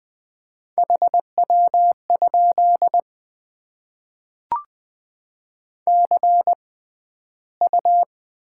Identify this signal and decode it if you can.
Morse code, heard as a keyed tone, about 20 wpm, spaced as for 12 wpm, 700 Hz. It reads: HW? CU
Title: Top 20 QSO Elements - 6x Character Spacing - 20wpm